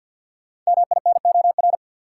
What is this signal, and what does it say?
Morse code, heard as a keyed tone, about 47 wpm, 700 Hz. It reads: MINOR
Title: Sentences from Top 1500 Words - New Word Drill Alt - 50wpm